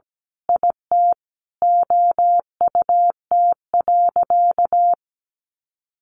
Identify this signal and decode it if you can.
Morse code, heard as a keyed tone, about 17 wpm, 700 Hz. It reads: IT OUT.
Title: Sentences from Top 1400 Words 17wpm